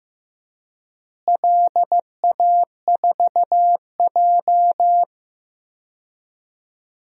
Morse code, heard as a keyed tone, about 16 wpm, 700 Hz. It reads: LA4J
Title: DXCC-100 Call Signs - 4 Characters - Repeated 2x - 15wpm